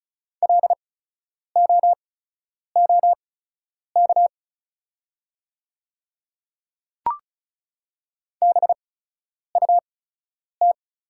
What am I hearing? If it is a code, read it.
Morse code, heard as a keyed tone, about 36 wpm, spaced as for 10 wpm, 700 Hz. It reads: LOOK BUT
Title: 8x Character Spacing - Top 100 Words - 35wpm